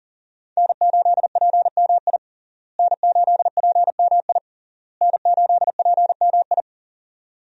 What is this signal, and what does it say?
Morse code, heard as a keyed tone, about 40 wpm, 700 Hz. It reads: N8PMI N8PMI N8PMI
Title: USA Call Signs - 5 Characters - Repeated 3x - 40wpm